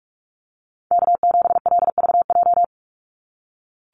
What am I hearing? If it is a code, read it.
Morse code, heard as a keyed tone, about 44 wpm, 700 Hz. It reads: K7LVJ
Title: Call Signs - 45wpm